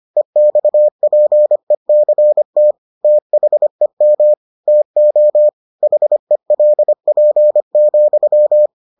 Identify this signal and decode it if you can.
Morse code, heard as a keyed tone, about 25 wpm, 600 Hz. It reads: EXPECT THEM TO HELP,